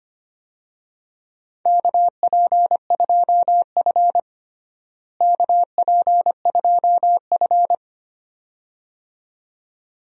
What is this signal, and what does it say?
Morse code, heard as a keyed tone, about 25 wpm, 700 Hz. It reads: KP2F KP2F
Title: DXCC-100 Call Signs - 4 Characters - Repeated 2x - 25wpm